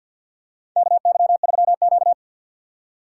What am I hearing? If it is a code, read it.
Morse code, heard as a keyed tone, about 53 wpm, 700 Hz. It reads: KY3Q
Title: USA Call Signs - 4 Characters - 2x Character Spacing - 50wpm